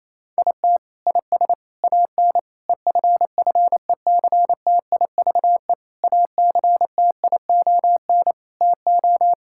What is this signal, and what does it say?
Morse code, heard as a keyed tone, about 28 wpm, 700 Hz. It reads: IT IS AN EFFECTIVE ACTION TO
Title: Sentences from Top 1000 Words - Review Part 6 - 28wpm